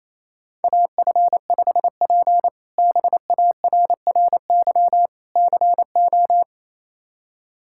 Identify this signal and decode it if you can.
Morse code, heard as a keyed tone, about 27 wpm, 700 Hz. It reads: AF5P BARRY CO
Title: K1USN SST QSO Practice - 28wpm